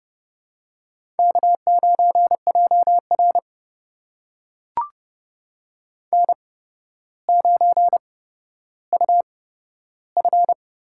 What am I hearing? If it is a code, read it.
Morse code, heard as a keyed tone, about 30 wpm, spaced as for 13 wpm, 700 Hz. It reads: K9JR N9UF
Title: USA Call Signs - 4 Characters - 8x Character Spacing - 30wpm